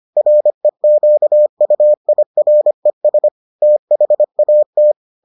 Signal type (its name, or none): Morse code